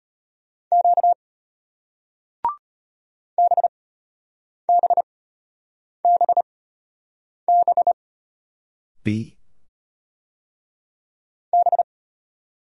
Morse code, heard as a keyed tone, about 34 wpm, spaced as for 8 wpm, 700 Hz. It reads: Q BBBB B